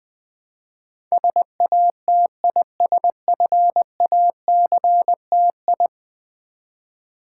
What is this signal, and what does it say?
Morse code, heard as a keyed tone, about 21 wpm, 700 Hz. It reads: SATISFACTI
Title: Build A Word - 5 Letters At A Time - 12-Letter Words - 20wpm